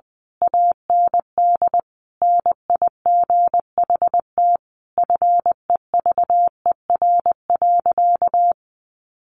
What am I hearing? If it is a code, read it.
Morse code, heard as a keyed tone, about 20 wpm, 700 Hz. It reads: AND NIGHT FEVER.
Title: A Sound Of Thunder 20wpm